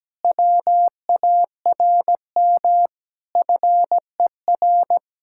Morse code, heard as a keyed tone, about 17 wpm, 700 Hz. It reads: WARM FER
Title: Speed-Racing Common QSO Phrases - 17wpm